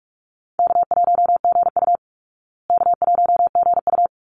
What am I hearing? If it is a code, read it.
Morse code, heard as a keyed tone, about 44 wpm, 700 Hz. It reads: K1GU K1GU
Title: Easy Call Signs Repeated 2x - 45wpm